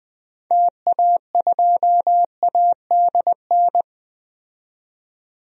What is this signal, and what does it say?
Morse code, heard as a keyed tone, about 20 wpm, 700 Hz. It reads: TA2ADN
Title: DXCC-100 Call Signs - 6 Characters - 20wpm